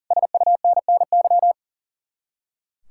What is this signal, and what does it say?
Morse code, heard as a keyed tone, about 49 wpm, 700 Hz. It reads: SUNNY
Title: Speed-Racing Common QSO Phrases - 40wpm